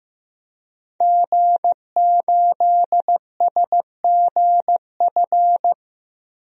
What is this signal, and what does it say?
Morse code, heard as a keyed tone, about 15 wpm, 700 Hz. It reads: G8SGF
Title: DXCC-100 Call Signs - 5 Characters - 15wpm